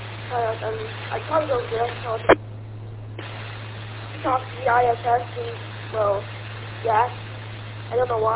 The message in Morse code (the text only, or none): none